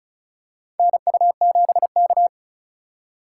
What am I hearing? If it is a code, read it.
Morse code, heard as a keyed tone, about 36 wpm, 700 Hz. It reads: NU7K
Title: USA Call Signs - 4 Characters - 3x Character Spacing - 35wpm